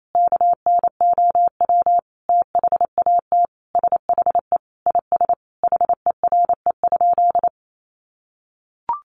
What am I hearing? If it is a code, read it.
Morse code, heard as a keyed tone, about 28 wpm, 700 Hz. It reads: KNOW THAT SHE IS HERE?
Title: Sentences from Top 1300 Words 28wpm